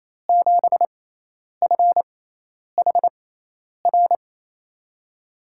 Morse code, heard as a keyed tone, about 28 wpm, spaced as for 10 wpm, 700 Hz. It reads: 7FHR
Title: USA Call Signs - 6x Character Spacing - 28wpm